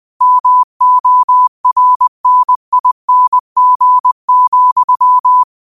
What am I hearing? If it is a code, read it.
Morse code, heard as a keyed tone, about 20 wpm, 1000 Hz. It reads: MORNING,